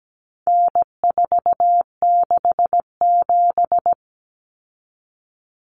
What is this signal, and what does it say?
Morse code, heard as a keyed tone, about 17 wpm, 700 Hz. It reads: N467
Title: Common Cut Numbers - 4-Digit 17wpm